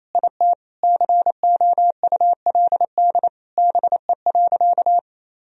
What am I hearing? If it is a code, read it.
Morse code, heard as a keyed tone, about 28 wpm, 700 Hz. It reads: IT COULD BE.
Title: Sentences from Top 1000 Words - Review Part 3 - 28wpm